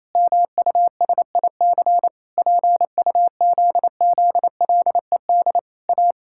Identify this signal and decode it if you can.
Morse code, heard as a keyed tone, about 28 wpm, 700 Hz. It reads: MUSIC PUZZLED A
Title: Pangram Challenge v2 - 28wpm